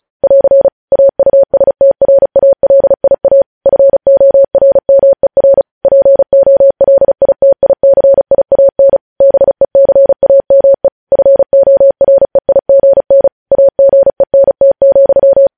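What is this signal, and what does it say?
Morse code, heard as a keyed tone, about 36 wpm, 550 Hz. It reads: <AR> AUSTRALIA FORMER POLITICIAN BECAME FOREIGN AGENT,
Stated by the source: Morse code news headlines sent at 35 wpm.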